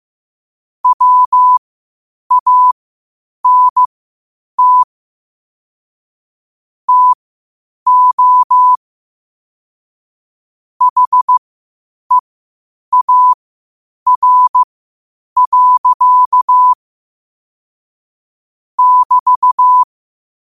Morse code, heard as a keyed tone, about 15 wpm, spaced as for 8 wpm, 1000 Hz. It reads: WANT TO HEAR. <BT>
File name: qotdfiles / 8wpm